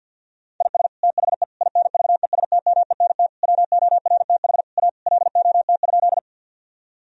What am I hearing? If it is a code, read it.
Morse code, heard as a keyed tone, about 50 wpm, 700 Hz. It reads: IS THE INVESTMENT WORTH A LOT?
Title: Sentences from Top 1000 Words - Review Part 4 - 50wpm